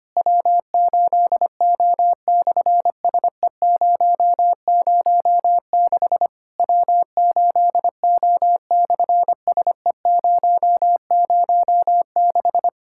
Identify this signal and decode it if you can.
Morse code, heard as a keyed tone, about 25 wpm, 700 Hz. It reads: W8O/SE006 W8O/SE006
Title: SOTA QSO Practice 25wpm